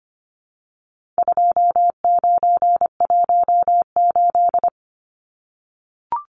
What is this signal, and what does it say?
Morse code, heard as a keyed tone, about 25 wpm, 700 Hz. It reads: 2918
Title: Numbers - 4-Digit 25wpm